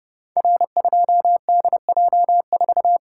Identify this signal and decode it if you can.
Morse code, heard as a keyed tone, about 30 wpm, 700 Hz. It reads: R2DJ4
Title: Code Groups - 5 Letter:Numbers - 30wpm